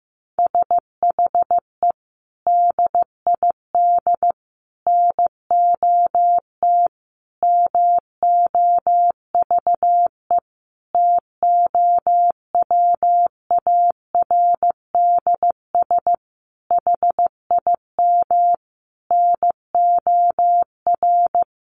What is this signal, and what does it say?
Morse code, heard as a keyed tone, about 14 wpm, 700 Hz. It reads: SHE DID NOT MOVE TOWARDS HIM NOR
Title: Sentences from Top 1500 Words 15wpm